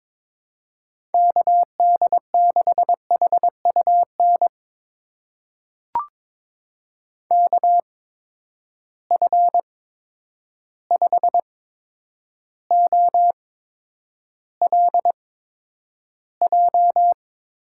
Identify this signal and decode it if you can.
Morse code, heard as a keyed tone, about 22 wpm, spaced as for 9 wpm, 700 Hz. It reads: KD6HUN KF5OLJ
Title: USA Call Signs - 6 Characters - 8x Character Spacing - 22wpm